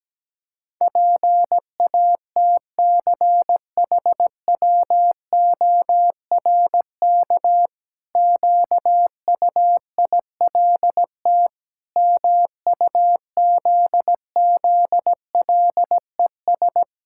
Morse code, heard as a keyed tone, about 17 wpm, 700 Hz. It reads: PATCHWORK QUILT MUZZLES
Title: Pangram Challenge v2 - 17wpm